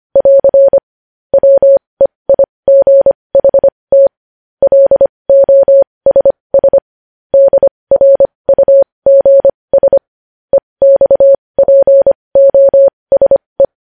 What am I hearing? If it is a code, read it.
Morse code, heard as a keyed tone, about 25 wpm, spaced as for 20 wpm, 550 Hz. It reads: <AR> WEIGHT LOSS DRUGS EXPOSE
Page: news-headlines-in-morse-code-at-20-wpm